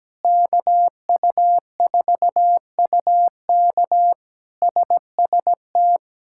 Morse code, heard as a keyed tone, about 17 wpm, 700 Hz. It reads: KU4UK SST
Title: K1USN SST QSO Practice - 17wpm